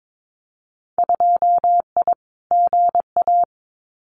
Morse code, heard as a keyed tone, about 22 wpm, 700 Hz. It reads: 2I GA
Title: Winter Field Day Exchanges 22wpm